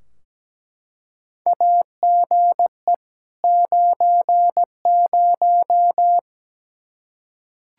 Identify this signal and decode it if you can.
Morse code, heard as a keyed tone, about 17 wpm, 700 Hz. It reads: AGE 90
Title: Practice QSO - Exchange 3 - More Info - 17wpm